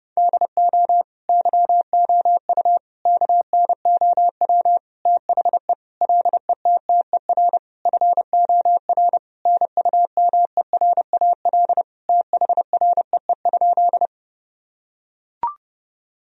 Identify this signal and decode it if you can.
Morse code, heard as a keyed tone, about 32 wpm, 700 Hz. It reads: DO YOU KNOW THE LETTER FOR NUMERAL THREE?
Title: Sentences from Top 1000 Words - Review Part 2 - 30wpm